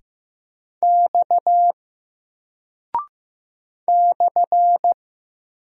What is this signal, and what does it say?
Morse code, heard as a keyed tone, about 15 wpm, 700 Hz. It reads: X /
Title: Single Character (Letter:Number:Punctuation:Prosign) 15wpm